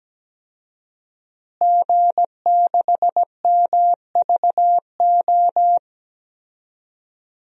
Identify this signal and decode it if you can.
Morse code, heard as a keyed tone, about 17 wpm, 700 Hz. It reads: G6MVO
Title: DXCC-100 Call Signs - 5 Characters - Repeated 2x - 17wpm